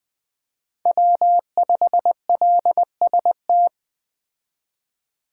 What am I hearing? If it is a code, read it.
Morse code, heard as a keyed tone, about 20 wpm, 700 Hz. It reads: W5LST
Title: 2023 POTA Activator Call Signs - 20wpm